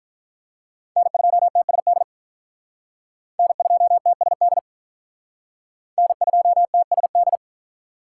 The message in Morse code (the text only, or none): N2TSD N2TSD N2TSD